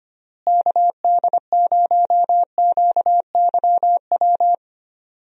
Morse code, heard as a keyed tone, about 25 wpm, 700 Hz. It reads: KD0QYW
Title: USA Call Signs - 6 Characters - Repeated 2x - 25wpm